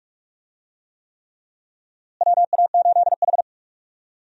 Morse code, heard as a keyed tone, about 46 wpm, 700 Hz. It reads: WA8H
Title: USA Call Signs - 4 Characters - Repeated 3x - 45wpm